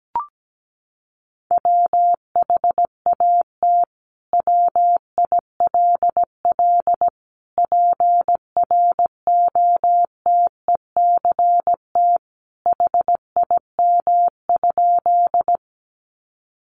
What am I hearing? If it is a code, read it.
Morse code, heard as a keyed tone, about 17 wpm, 700 Hz. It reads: WHAT WILL PROTECT HIM?
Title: Sentences from Top 900 Words 17wpm